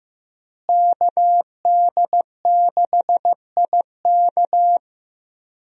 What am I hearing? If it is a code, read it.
Morse code, heard as a keyed tone, about 15 wpm, 700 Hz. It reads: KD6IK
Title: USA Call Signs - 5 Characters - Repeated 2x - 15wpm